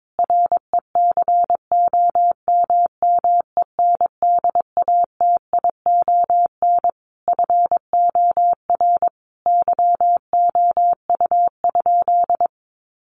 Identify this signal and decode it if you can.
Morse code, heard as a keyed tone, about 22 wpm, 700 Hz. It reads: RECOMMENDATION FOR YOU?
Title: Sentences from Top 1400 Words 22wpm